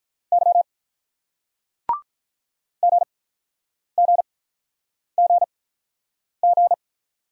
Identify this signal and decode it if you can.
Morse code, heard as a keyed tone, about 47 wpm, spaced as for 9 wpm, 700 Hz. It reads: Y GGGG